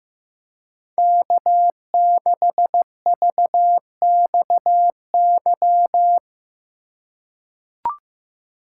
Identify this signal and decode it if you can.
Morse code, heard as a keyed tone, about 15 wpm, 700 Hz. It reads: K6VXY
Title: USA Call Signs - 5 Characters - 3x Character Spacing - 15wpm